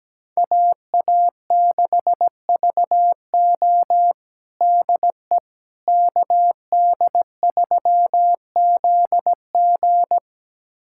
Morse code, heard as a keyed tone, about 17 wpm, 700 Hz. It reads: AA6VO DE KD3ZG
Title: USA Callsign DE Callsign - 5 Characters - 17wpm